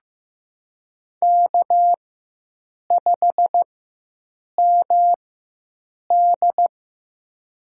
Morse code, heard as a keyed tone, about 15 wpm, spaced as for 7 wpm, 700 Hz. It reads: K5MD
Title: USA Call Signs - 4 Characters - 4x Character Spacing - 15wpm